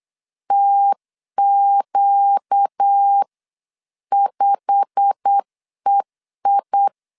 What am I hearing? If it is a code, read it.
Morse code, heard as a keyed tone, about 8 wpm, 800 Hz. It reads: TQ5EI
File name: CW / DET / 1 9 1500C S A 8wpm 5637